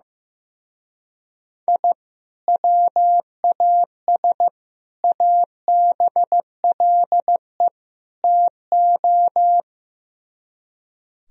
Morse code, heard as a keyed tone, about 15 wpm, 700 Hz. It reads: I WAS ABLE TO
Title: Speed-Racing - Sets of 4 Words 15wpm